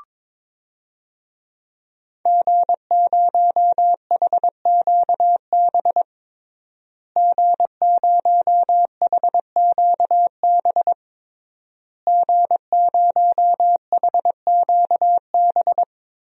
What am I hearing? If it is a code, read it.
Morse code, heard as a keyed tone, about 22 wpm, 700 Hz. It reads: G0HQB G0HQB G0HQB
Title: DXCC-100 Call Signs - 5 Characters - Repeated 3x - 22wpm